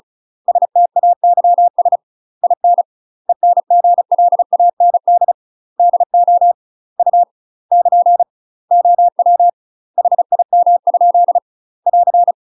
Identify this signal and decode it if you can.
Morse code, heard as a keyed tone, about 36 wpm, 700 Hz. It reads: STAYS IN ENGLAND DO U <KN> OW HIM? <AR>
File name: UK MASTER QSOs TRAINING / 35WPM / UK MASTER QSOs 0035